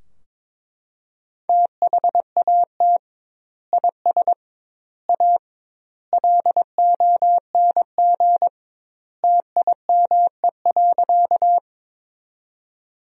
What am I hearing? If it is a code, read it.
Morse code, heard as a keyed tone, about 22 wpm, 700 Hz. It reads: THAT IS A LONG TIME.
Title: 2x Word-Spacing - Sentences from Top 100 Words - 22wpm